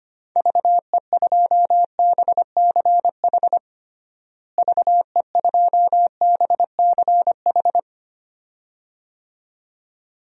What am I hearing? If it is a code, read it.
Morse code, heard as a keyed tone, about 25 wpm, 700 Hz. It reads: VE2BCH VE2BCH
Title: DXCC-100 Call Signs - 6 Characters - Repeated 2x - 25wpm